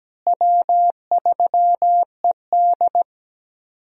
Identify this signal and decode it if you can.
Morse code, heard as a keyed tone, about 17 wpm, 700 Hz. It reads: W3ED
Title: USA Call Signs - 4 Characters - Repeated 2x - 17wpm